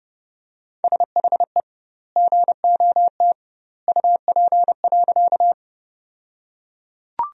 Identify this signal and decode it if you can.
Morse code, heard as a keyed tone, about 30 wpm, 700 Hz. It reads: SHE GOT UP.
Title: 2x Word-Spacing - Sentences from Top 300 Words - 30wpm